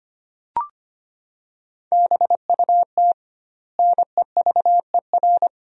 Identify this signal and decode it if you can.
Morse code, heard as a keyed tone, about 25 wpm, 700 Hz. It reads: BUT NEVER
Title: 2x Word-Spacing - Sentences from Top 500 Words - 25wpm